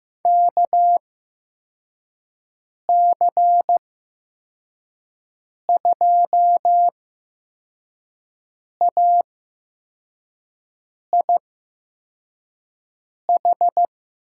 Morse code, heard as a keyed tone, about 15 wpm, spaced as for 4 wpm, 700 Hz. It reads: KC2AIH
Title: USA Call Signs - 8x Character Spacing - 15wpm